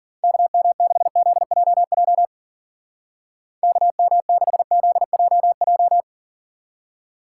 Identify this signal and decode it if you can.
Morse code, heard as a keyed tone, about 50 wpm, 700 Hz. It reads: KM6ZJJ KM6ZJJ
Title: Speed Racing - USA 2x3 Call Signs 40wpm